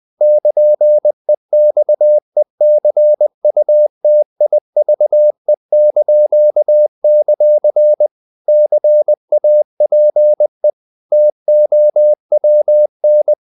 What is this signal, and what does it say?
Morse code, heard as a keyed tone, about 20 wpm, 600 Hz. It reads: <KN>EXECUTIVE); CAPE TOWN
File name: CW / Practice / NationalCapitals20